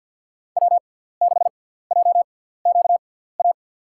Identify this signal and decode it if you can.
Morse code, heard as a keyed tone, about 52 wpm, spaced as for 17 wpm, 690 Hz. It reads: W6JQA
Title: USA Call Signs - 5 Characters - 6x Character Spacing - 50wpm